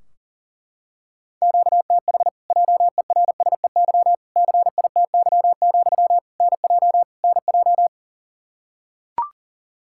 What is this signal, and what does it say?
Morse code, heard as a keyed tone, about 37 wpm, 700 Hz. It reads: QTH JERSEY CITY, NJ NJ
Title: Practice QSO - Exchange 1 - Beginning - 40wpm